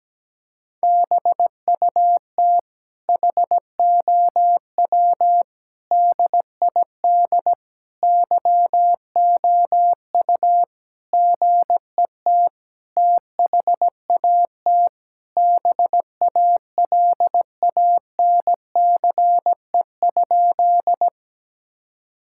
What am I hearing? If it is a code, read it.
Morse code, heard as a keyed tone, about 17 wpm, 700 Hz. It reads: BUT HOW DID YOU GET THAT BALANCE?
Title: Sentences from Top 1000 Words - Review Part 6 - 17wpm